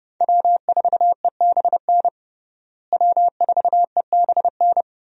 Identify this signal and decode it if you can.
Morse code, heard as a keyed tone, about 32 wpm, 700 Hz. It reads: W4EBN W4EBN
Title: USA Call Signs - 5 Characters - Repeated 2x - 30wpm